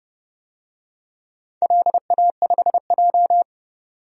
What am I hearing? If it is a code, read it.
Morse code, heard as a keyed tone, about 30 wpm, 700 Hz. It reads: LA5J
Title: DXCC-100 Call Signs - 4 Characters - 30wpm